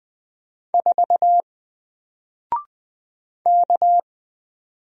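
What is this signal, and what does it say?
Morse code, heard as a keyed tone, about 20 wpm, 700 Hz. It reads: 4 K
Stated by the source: Single letter and number Morse drill at 20 wpm, from N4LCD.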